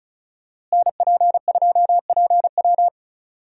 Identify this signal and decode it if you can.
Morse code, heard as a keyed tone, about 34 wpm, 700 Hz. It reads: NP2PW
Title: USA Call Signs - 5 Characters - Repeated 3x - 35wpm